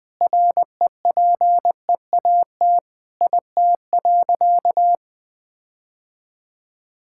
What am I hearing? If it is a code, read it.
Morse code, heard as a keyed tone, about 20 wpm, 700 Hz. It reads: REPEAT IT.
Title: Sentences from Top 1600 Words 20wpm